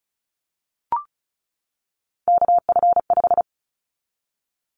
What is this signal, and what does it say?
Morse code, heard as a keyed tone, about 34 wpm, 700 Hz. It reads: KF5
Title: Callsign Prefixes for Top 25 DXCC - 35wpm